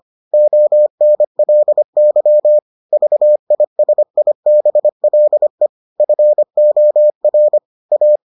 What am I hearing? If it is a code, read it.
Morse code, heard as a keyed tone, about 25 wpm, 600 Hz. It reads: ONLY VISIBLE FOR A